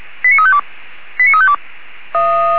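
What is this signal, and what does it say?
Five-tone selective call (ZVEI), heard as quick runs of five tones, tones of about 70 ms. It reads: 89352 89352
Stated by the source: Five-tone ZVEI call received on SDR, fire brigade Feuerwehr Orlamünde.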